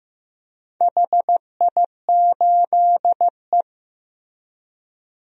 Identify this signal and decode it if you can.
Morse code, heard as a keyed tone, about 15 wpm, 700 Hz. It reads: HI8E
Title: DXCC-100 Call Signs - 4 Characters - Repeated 2x - 15wpm